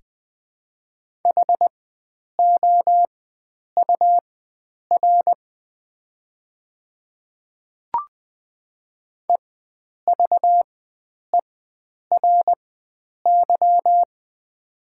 Morse code, heard as a keyed tone, about 20 wpm, spaced as for 9 wpm, 700 Hz. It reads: HOUR EVERY
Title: 4x Character Spacing - Top 500 Words - 20wpm